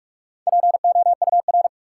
Morse code, heard as a keyed tone, about 42 wpm, 700 Hz. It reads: POUR
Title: Four Letter Words - 45wpm